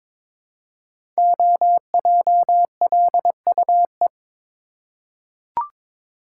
Morse code, heard as a keyed tone, about 22 wpm, 700 Hz. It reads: OJLUE
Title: Code Groups - 5 Letter:Numbers - 22wpm